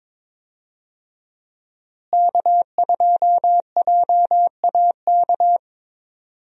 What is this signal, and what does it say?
Morse code, heard as a keyed tone, about 22 wpm, 700 Hz. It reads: K2JAK
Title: USA Call Signs - 5 Characters - Repeated 2x - 22wpm